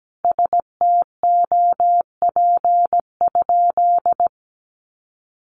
Morse code, heard as a keyed tone, about 17 wpm, 700 Hz. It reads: STOP?
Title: Sentences from Top 1300 Words 17wpm